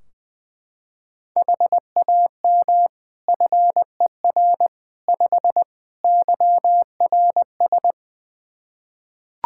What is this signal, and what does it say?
Morse code, heard as a keyed tone, about 20 wpm, 700 Hz. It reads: HAM FER 5 YRS
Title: K1USN SST QSO Practice 20wpm